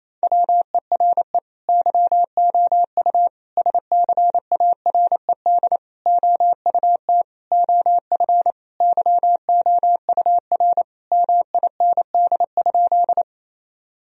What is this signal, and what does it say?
Morse code, heard as a keyed tone, about 27 wpm, 700 Hz. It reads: WERE YOU SCARED OUT OF YOUR MIND?
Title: Sentences from Top 1000 Words - Review Part 4 - 28wpm